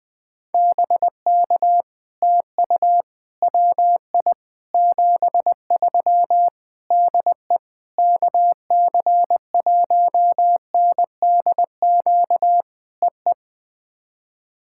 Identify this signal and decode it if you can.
Morse code, heard as a keyed tone, about 20 wpm, 700 Hz. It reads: BK TU WI 73 DE KC1NDQ EE
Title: POTA QSO Practice - 20wpm